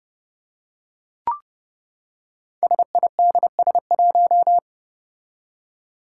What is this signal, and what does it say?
Morse code, heard as a keyed tone, about 30 wpm, 700 Hz. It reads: SIDS1
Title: Five Challenging Letter-Number Combinations 30wpm